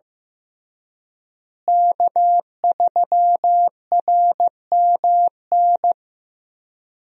Morse code, heard as a keyed tone, about 15 wpm, 700 Hz. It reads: K3RMN
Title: USA Call Signs - 5 Characters - Repeated 3x - 15wpm